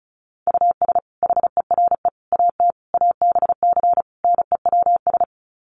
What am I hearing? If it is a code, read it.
Morse code, heard as a keyed tone, about 34 wpm, 700 Hz. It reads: US HERE AT ABC NEWS
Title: Sets of 5 Words 35wpm